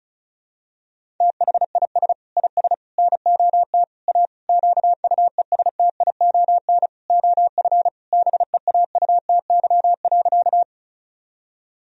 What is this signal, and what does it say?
Morse code, heard as a keyed tone, about 34 wpm, 700 Hz. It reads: THIS IS NOT A QUESTION OF BEAUTY.
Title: Sentences from Top 1000 Words - Review Part 2 - 35wpm